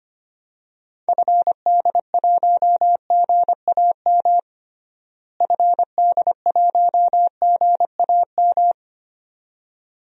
Morse code, heard as a keyed tone, about 25 wpm, 700 Hz. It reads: FD1GAM FD1GAM
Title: DXCC-100 Call Signs - 6 Characters - Repeated 2x - 25wpm